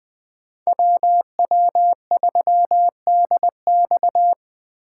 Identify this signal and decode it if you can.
Morse code, heard as a keyed tone, about 25 wpm, 700 Hz. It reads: WW3DX
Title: Speed Racing - USA 2x2 Call Signs 20wpm